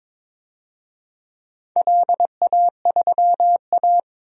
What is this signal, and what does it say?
Morse code, heard as a keyed tone, about 22 wpm, 700 Hz. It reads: LA3A
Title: DXCC-100 Call Signs - 4 Characters - 22wpm